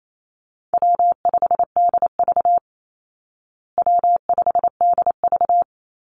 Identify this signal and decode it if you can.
Morse code, heard as a keyed tone, about 28 wpm, 700 Hz. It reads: W5DV W5DV
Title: Call Signs Repeated 3x - 28wpm